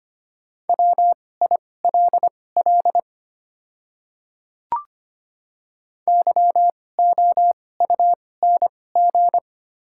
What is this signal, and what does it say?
Morse code, heard as a keyed tone, about 25 wpm, spaced as for 18 wpm, 700 Hz. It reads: WILL YOUNG
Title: 2x Character Spacing - Top 500 Words - 25wpm